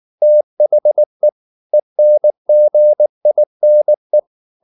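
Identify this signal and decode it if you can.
Morse code, heard as a keyed tone, about 19 wpm, 600 Hz. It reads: THE ENGINE
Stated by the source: Morse code at 19 wpm from CW med Gnister, reading Ken Follett's novel Never in English.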